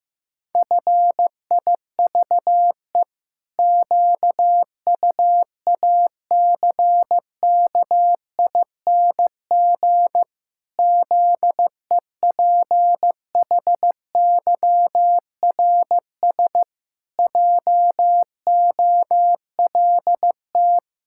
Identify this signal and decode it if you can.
Morse code, heard as a keyed tone, about 15 wpm, 700 Hz. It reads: FIVE QUACKING ZEPHYRS JOLT
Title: Pangram Challenge v2 - 15wpm